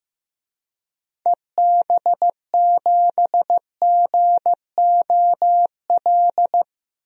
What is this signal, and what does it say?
Morse code, heard as a keyed tone, about 15 wpm, 700 Hz. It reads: EB7GOL
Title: DXCC-100 Call Signs - 6 Characters - 15wpm